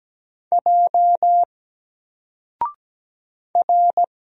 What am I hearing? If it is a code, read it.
Morse code, heard as a keyed tone, about 17 wpm, 700 Hz. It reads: J R
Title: Single Letter-Number Alt 17wpm